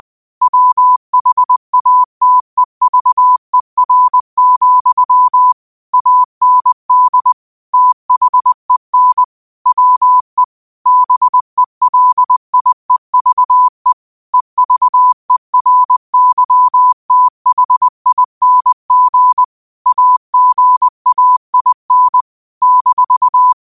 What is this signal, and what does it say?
Morse code, heard as a keyed tone, about 20 wpm, 1000 Hz. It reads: WHATEVER, AND THEN WE BELIEVE EVERYTHING AGAIN -